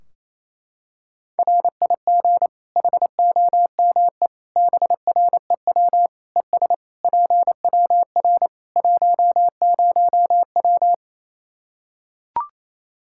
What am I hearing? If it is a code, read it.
Morse code, heard as a keyed tone, about 28 wpm, 700 Hz. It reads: RIG HOME BREW ES PWR 10W
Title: Practice QSO - Exchange 2 - Info - 28wpm